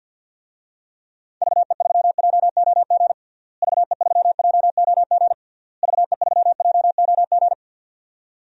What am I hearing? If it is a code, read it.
Morse code, heard as a keyed tone, about 52 wpm, 700 Hz. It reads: VE3JOG VE3JOG VE3JOG
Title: DXCC-100 Call Signs - 6 Characters - Repeated 3x - 50wpm